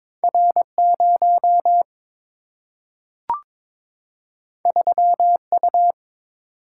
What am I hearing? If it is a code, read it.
Morse code, heard as a keyed tone, about 22 wpm, 700 Hz. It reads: R 0 3 U
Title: Single-Letter Single-Number 22wpm